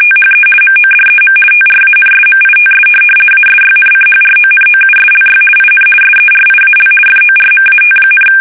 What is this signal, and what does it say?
RTTY, about 50 baud. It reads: THE MODEM TRANSMITS ASYNCHRONOUSLY IN SERIAL FORMAT,
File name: fsk850-50B / teletype-corp